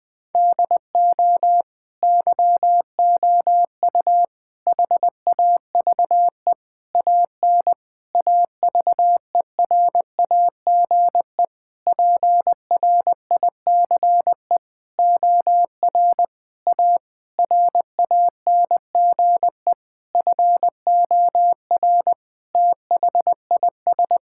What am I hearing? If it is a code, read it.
Morse code, heard as a keyed tone, about 20 wpm, 700 Hz. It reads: DO YOU HAVE AN AVERAGE PRICE OR A RANGE FOR THIS
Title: Sentences with 9-Word Phrases - 20wpm